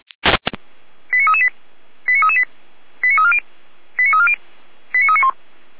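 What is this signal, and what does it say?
Five-tone selective call (ZVEI), heard as quick runs of five tones, tones of about 70 ms. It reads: 89338 89338 89340 89340 89391
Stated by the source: Five-tone ZVEI selective call received on SDR.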